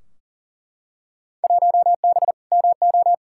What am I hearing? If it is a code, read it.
Morse code, heard as a keyed tone, about 40 wpm, 700 Hz. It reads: 1B MO
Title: Field Day Exchanges - 40wpm